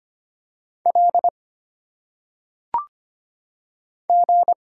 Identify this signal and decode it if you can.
Morse code, heard as a keyed tone, about 25 wpm, 700 Hz. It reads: L G
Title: Single Letters - Phonetic Pronunciation 25wpm